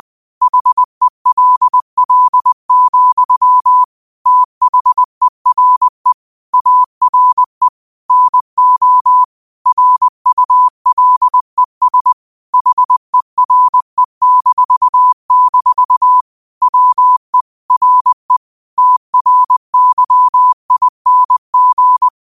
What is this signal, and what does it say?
Morse code, heard as a keyed tone, about 20 wpm, 1000 Hz. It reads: HELL, THERE ARE NO RULES HERE-- WE RE TRYING
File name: qotdfiles / 20wpm